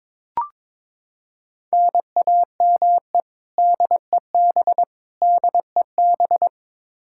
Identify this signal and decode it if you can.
Morse code, heard as a keyed tone, about 22 wpm, 700 Hz. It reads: NAME DEB DEB
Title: Practice QSO - Exchange 1 - Beginning - 22wpm